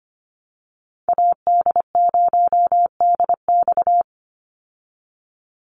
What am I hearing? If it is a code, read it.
Morse code, heard as a keyed tone, about 25 wpm, 700 Hz. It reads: AD0DX
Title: Call Signs Repeated 2x - 25wpm